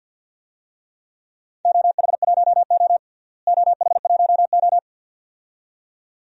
Morse code, heard as a keyed tone, about 49 wpm, 700 Hz. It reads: OH1O OH1O
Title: DXCC-100 Call Signs - 4 Characters - Repeated 2x - 50wpm